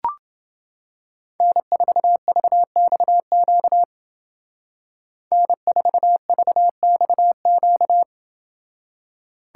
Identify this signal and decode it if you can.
Morse code, heard as a keyed tone, about 29 wpm, 700 Hz. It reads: N4VXQ N4VXQ